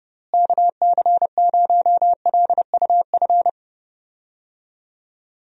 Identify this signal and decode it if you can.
Morse code, heard as a keyed tone, about 30 wpm, 700 Hz. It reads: KC0LUF
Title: USA Call Signs - 6 Characters - Repeated 2x - 30wpm